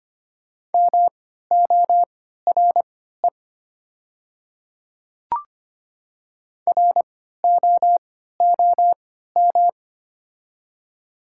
Morse code, heard as a keyed tone, about 25 wpm, spaced as for 14 wpm, 700 Hz. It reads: MORE ROOM
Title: 3x Character Spacing - Top 300 Words - 25wpm